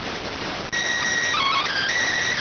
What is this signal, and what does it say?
SSTV (slow-scan television), PD 290.